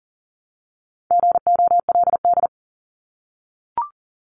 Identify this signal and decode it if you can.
Morse code, heard as a keyed tone, about 40 wpm, 700 Hz. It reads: GOLD
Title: Top 500 Words 40wpm